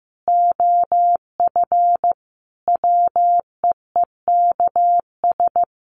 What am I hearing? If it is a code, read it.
Morse code, heard as a keyed tone, about 15 wpm, 700 Hz. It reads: OF WEEKS
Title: Sets of 5 Words 15wpm